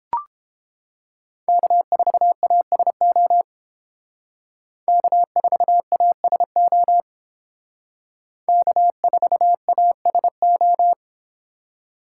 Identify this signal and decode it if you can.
Morse code, heard as a keyed tone, about 30 wpm, 700 Hz. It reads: K4ASO K4ASO K4ASO